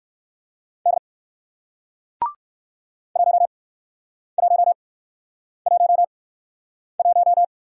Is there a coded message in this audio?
Yes: Morse code from 0.9 to 7.4 s, heard as a keyed tone, about 59 wpm, spaced as for 9 wpm, 700 Hz. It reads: D 1111